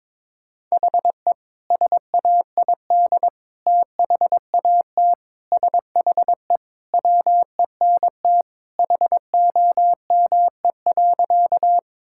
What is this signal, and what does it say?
Morse code, heard as a keyed tone, about 21 wpm, 700 Hz. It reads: HE SAID THAT SHE WENT HOME.
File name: practice / 22wpm / Sentences with 2-Word Phrases - 22wpm